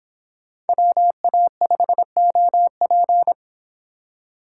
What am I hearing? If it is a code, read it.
Morse code, heard as a keyed tone, about 31 wpm, 700 Hz. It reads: WA5OP